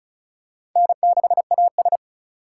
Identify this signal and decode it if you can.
Morse code, heard as a keyed tone, about 34 wpm, 700 Hz. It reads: N6AS
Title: USA Call Signs - 4 Characters - 2x Character Spacing - 35wpm